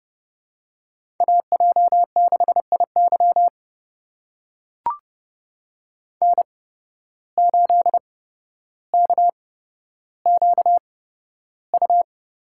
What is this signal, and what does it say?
Morse code, heard as a keyed tone, about 30 wpm, spaced as for 13 wpm, 700 Hz. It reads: AJ6IY N8KQU
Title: USA Call Signs - 5 Characters - 8x Character Spacing - 30wpm